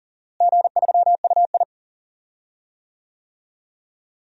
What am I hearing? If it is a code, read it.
Morse code, heard as a keyed tone, about 40 wpm, 700 Hz. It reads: G3UI
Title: DXCC-100 Call Signs - 4 Characters - Repeated 2x - 40wpm